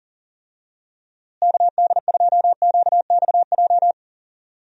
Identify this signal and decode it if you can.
Morse code, heard as a keyed tone, about 40 wpm, 700 Hz. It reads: KD2QXJ
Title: USA Call Signs - 6 Characters - Repeated 2x - 40wpm